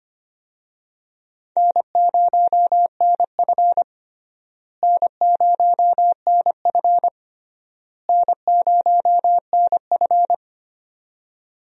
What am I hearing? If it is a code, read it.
Morse code, heard as a keyed tone, about 25 wpm, 700 Hz. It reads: N0NF N0NF N0NF
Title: USA Call Signs - 4 Characters - Repeated 3x - 25wpm